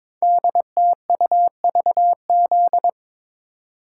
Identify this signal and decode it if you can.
Morse code, heard as a keyed tone, about 22 wpm, 700 Hz. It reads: DTUVZ
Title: Code Groups - 5 Letters - 22wpm